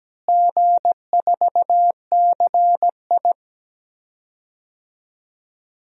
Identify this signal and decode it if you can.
Morse code, heard as a keyed tone, about 17 wpm, 700 Hz. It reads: G4CI
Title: DXCC-100 Call Signs - 4 Characters - Repeated 2x - 17wpm